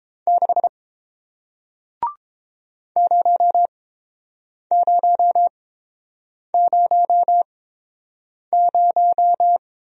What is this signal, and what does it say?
Morse code, heard as a keyed tone, about 31 wpm, spaced as for 7 wpm, 700 Hz. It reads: 6 0000